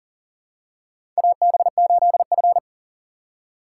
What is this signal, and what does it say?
Morse code, heard as a keyed tone, about 40 wpm, 700 Hz. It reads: AB8F
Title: USA Call Signs - 2x Character Spacing - 40wpm